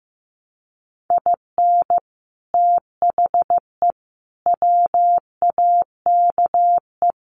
Morse code, heard as a keyed tone, about 15 wpm, 700 Hz. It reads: IN THE WAKE
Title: Sets of 4 Words 15wpm